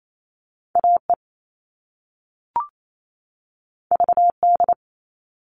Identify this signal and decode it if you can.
Morse code, heard as a keyed tone, about 28 wpm, 700 Hz. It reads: AE VD
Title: All Cut Numbers - 2-Digit 28wpm